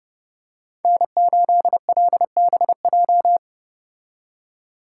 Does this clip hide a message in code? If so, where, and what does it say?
Yes: Morse code from 0.8 to 3.4 s, heard as a keyed tone, about 29 wpm, 700 Hz. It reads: N8LBJ